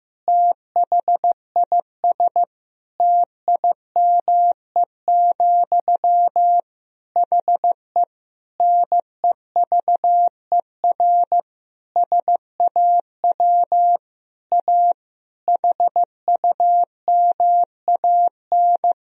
Morse code, heard as a keyed tone, about 15 wpm, 700 Hz. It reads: THIS TIME, HE NEVER SAW A HUMAN
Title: Sentences from Top 1000 Words - Review Part 3 - 15wpm